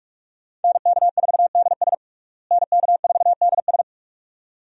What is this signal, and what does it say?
Morse code, heard as a keyed tone, about 42 wpm, 700 Hz. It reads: NK4DS NK4DS
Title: USA Call Signs - 5 Characters - Repeated 2x - 45wpm